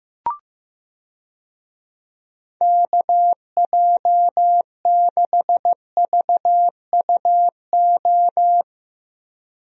Morse code, heard as a keyed tone, about 15 wpm, 700 Hz. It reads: KJ6VUO
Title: USA Call Signs - 6 Characters - Repeated 3x - 15wpm